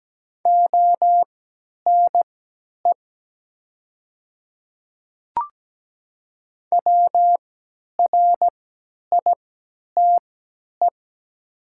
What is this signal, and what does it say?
Morse code, heard as a keyed tone, about 17 wpm, spaced as for 10 wpm, 700 Hz. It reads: ONE WRITE
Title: 3x Character Spacing - Top 500 Words - 17wpm